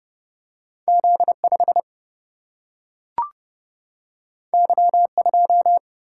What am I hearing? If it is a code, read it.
Morse code, heard as a keyed tone, about 29 wpm, 700 Hz. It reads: Z5 Y2
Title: Single-Letter Single-Number 30wpm